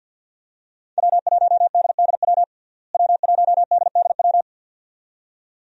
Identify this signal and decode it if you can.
Morse code, heard as a keyed tone, about 53 wpm, 700 Hz. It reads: W1DDW W1DDW
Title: USA Call Signs - 5 Characters - Repeated 2x - 50wpm